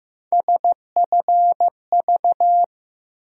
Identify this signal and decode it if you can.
Morse code, heard as a keyed tone, about 15 wpm, 700 Hz. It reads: SFV
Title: USA Call Sign Postfixes - 3 Letters 15wpm